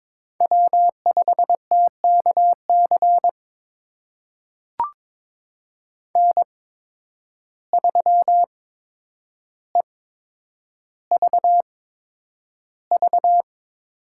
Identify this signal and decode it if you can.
Morse code, heard as a keyed tone, about 22 wpm, spaced as for 9 wpm, 700 Hz. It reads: W5TKC N3EVV
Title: USA Call Signs - 5 Characters - 8x Character Spacing - 22wpm